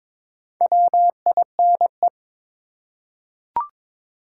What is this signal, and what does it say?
Morse code, heard as a keyed tone, about 22 wpm, 700 Hz. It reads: WINE
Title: Four Letter Words - 22wpm